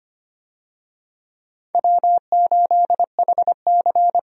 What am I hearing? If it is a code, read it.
Morse code, heard as a keyed tone, about 25 wpm, 700 Hz. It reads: W8HC
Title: USA Call Signs - 4 Characters - 25wpm